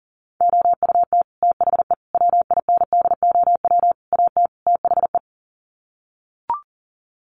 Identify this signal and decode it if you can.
Morse code, heard as a keyed tone, about 40 wpm, 700 Hz. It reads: OUT THE WINDOW AT THE
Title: Sets of 5 Words Encore2 40wpm